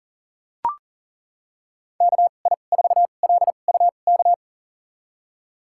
Morse code, heard as a keyed tone, about 37 wpm, spaced as for 28 wpm, 700 Hz. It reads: KI4LUK
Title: USA Call Signs - 6 Characters - 2x Character Spacing - 40wpm